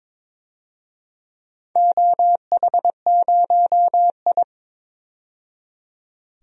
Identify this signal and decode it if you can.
Morse code, heard as a keyed tone, about 22 wpm, 700 Hz. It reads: OH0I
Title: DXCC-100 Call Signs - 4 Characters - 22wpm